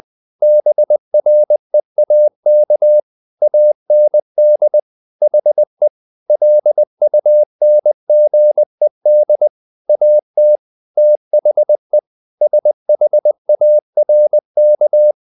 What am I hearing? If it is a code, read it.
Morse code, heard as a keyed tone, about 20 wpm, 600 Hz. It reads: BREAK AND HE LUNGED AT THE SHARK
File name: the old man and the sea part 2 20wpm